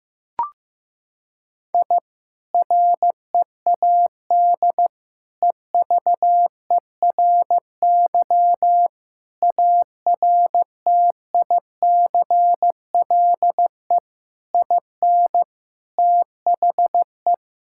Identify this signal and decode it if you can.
Morse code, heard as a keyed tone, about 16 wpm, 700 Hz. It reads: I READ EVERY ARTICLE IN THE
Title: Sentences from Top 1000 Words - Review Part 5 - 15wpm